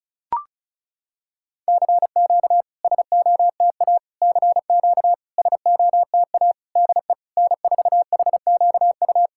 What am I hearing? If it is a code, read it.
Morse code, heard as a keyed tone, about 36 wpm, 690 Hz. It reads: CQ SOTA CQ SOTA DE N4HQU
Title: SOTA QSO Practice 35wpm